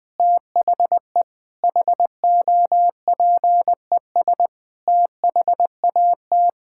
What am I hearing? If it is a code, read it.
Morse code, heard as a keyed tone, about 21 wpm, 700 Hz. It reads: THE HOPES THAT
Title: Sentences with 9-Word Phrases - Spoken Beginning - 20wpm